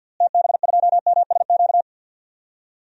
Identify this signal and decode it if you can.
Morse code, heard as a keyed tone, about 49 wpm, 700 Hz. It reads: TB2MSQ
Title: DXCC-100 Call Signs - 6 Characters - Repeated 3x - 50wpm